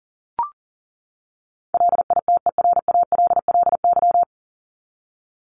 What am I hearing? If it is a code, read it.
Morse code, heard as a keyed tone, about 40 wpm, 700 Hz. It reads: LITERALLY
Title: Top 1251-1500 Words 40wpm